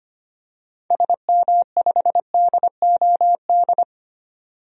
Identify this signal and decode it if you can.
Morse code, heard as a keyed tone, about 25 wpm, 700 Hz. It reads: SM5DOD
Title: DXCC-100 Call Signs - 6 Characters - 25wpm